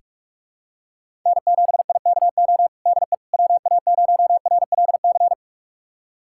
Morse code, heard as a keyed tone, about 46 wpm, 700 Hz. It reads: N7IKO DE WA0RLC
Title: USA Callsign DE Callsign 45wpm